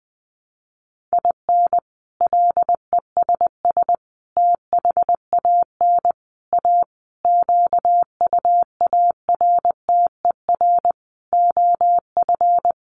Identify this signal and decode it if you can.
Morse code, heard as a keyed tone, about 20 wpm, 700 Hz. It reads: IN LESS THAN A QUARTER OF
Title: Sentences from Top 1300 Words 20wpm